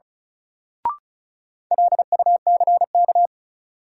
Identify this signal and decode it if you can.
Morse code, heard as a keyed tone, about 36 wpm, 700 Hz. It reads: LUCK
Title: Sentences from Top 1500 Words - New Word Drill - 35wpm